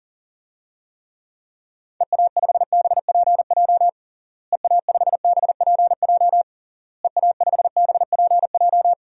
Morse code, heard as a keyed tone, about 40 wpm, 700 Hz. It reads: EA5BPJ EA5BPJ EA5BPJ
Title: DXCC-100 Call Signs - 6 Characters - Repeated 3x - 40wpm